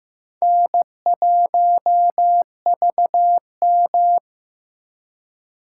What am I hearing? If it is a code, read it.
Morse code, heard as a keyed tone, about 15 wpm, 700 Hz. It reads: N1VM
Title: USA Call Signs - 4 Characters - 3x Character Spacing - 15wpm